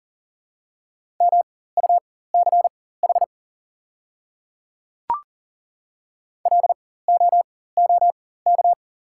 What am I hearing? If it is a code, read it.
Morse code, heard as a keyed tone, about 40 wpm, spaced as for 19 wpm, 690 Hz. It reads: MUCH LOOK
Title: 4x Character Spacing - Top 300 Words - 40wpm